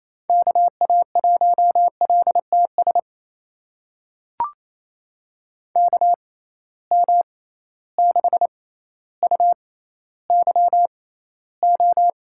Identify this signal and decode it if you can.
Morse code, heard as a keyed tone, about 28 wpm, spaced as for 14 wpm, 700 Hz. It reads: KA1LTS KM6UYO
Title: USA Call Signs - 6 Characters - 6x Character Spacing - 28wpm